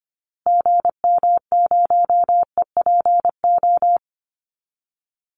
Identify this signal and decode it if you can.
Morse code, heard as a keyed tone, about 25 wpm, 700 Hz. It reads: GM0EPO
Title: Call Signs Repeated 2x - 25wpm